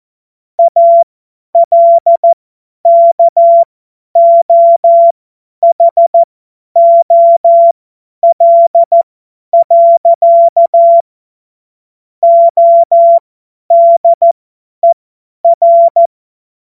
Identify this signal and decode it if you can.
Morse code, heard as a keyed tone, about 14 wpm, spaced as for 10 wpm, 680 Hz. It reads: ALKOHOL. ODER
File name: Morsekurs / Witze / Witz 27. Juli 2021, 50 F70